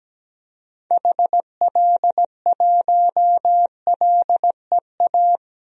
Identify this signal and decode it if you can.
Morse code, heard as a keyed tone, about 17 wpm, 700 Hz. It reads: HL1LEA
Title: DXCC-100 Call Signs - 6 Characters - 17wpm